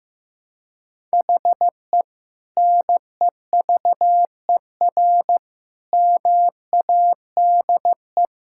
Morse code, heard as a keyed tone, about 15 wpm, 700 Hz. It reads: HE NEVER MADE
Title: Sentences from Top 1600 Words 15wpm